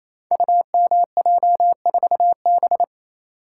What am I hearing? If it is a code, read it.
Morse code, heard as a keyed tone, about 27 wpm, 700 Hz. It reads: UMJ4B
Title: Code Groups - 5 Letter:Numbers - 28wpm